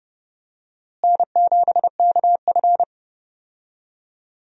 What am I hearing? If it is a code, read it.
Morse code, heard as a keyed tone, about 30 wpm, 700 Hz. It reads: N7KF
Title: USA Call Signs - 4 Characters - Repeated 3x - 30wpm